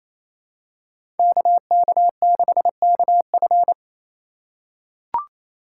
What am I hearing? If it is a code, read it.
Morse code, heard as a keyed tone, about 28 wpm, 700 Hz. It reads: KK6KF
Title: USA Call Signs - 5 Characters - 2x Character Spacing - 28wpm